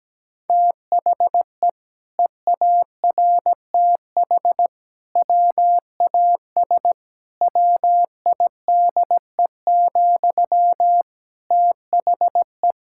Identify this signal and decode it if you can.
Morse code, heard as a keyed tone, about 17 wpm, 700 Hz. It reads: THE EARTH WAS WIDE, THE
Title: Sentences from Top 1000 Words - Review Part 6 - 17wpm